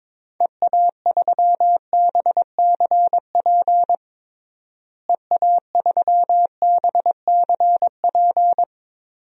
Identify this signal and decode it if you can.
Morse code, heard as a keyed tone, about 21 wpm, 700 Hz. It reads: EA3BCP EA3BCP
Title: DXCC-100 Call Signs - 6 Characters - Repeated 3x - 22wpm